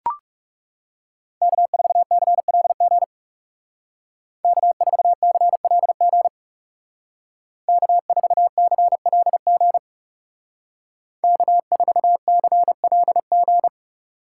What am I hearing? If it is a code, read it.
Morse code, heard as a keyed tone, about 36 wpm, 700 Hz. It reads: K4CLG K4CLG K4CLG K4CLG
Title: Speed Racing - USA Call Signs 30wpm